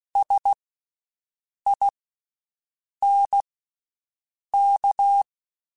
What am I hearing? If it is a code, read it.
Morse code, heard as a keyed tone, about 16 wpm, spaced as for 6 wpm, 800 Hz. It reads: SINK
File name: Special INGs / Special INGs-mp3s / VerySlowToFaster Ing @15